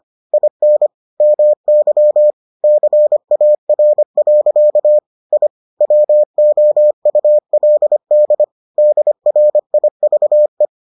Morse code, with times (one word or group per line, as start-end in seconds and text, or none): 0.3-0.9 IN
1.2-2.3 MY
2.6-5.0 CAR.
5.3-5.5 I
5.8-8.4 WOULD
8.8-10.7 DRIVE